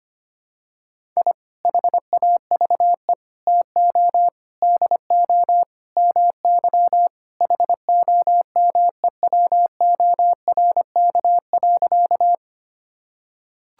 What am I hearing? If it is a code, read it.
Morse code, heard as a keyed tone, about 25 wpm, 700 Hz. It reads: I HAVE TO DO MY HOMEWORK.
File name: practice / 25wpm / Sentences with 3-Word Phrases - 25wpm